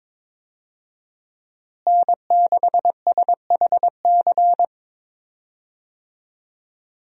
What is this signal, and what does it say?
Morse code, heard as a keyed tone, about 22 wpm, 700 Hz. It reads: N6SHC
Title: USA Call Signs - 5 Characters - Repeated 3x - 22wpm